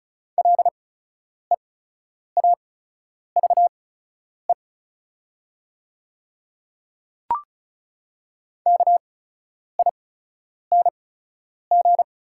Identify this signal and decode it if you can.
Morse code, heard as a keyed tone, about 36 wpm, spaced as for 10 wpm, 690 Hz. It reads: LEAVE KING
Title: 8x Character Spacing - Top 500 Words - 35wpm